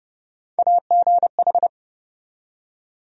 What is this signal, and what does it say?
Morse code, heard as a keyed tone, about 29 wpm, 700 Hz. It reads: AGH
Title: USA Call Sign Postfixes - 3 Letters 30wpm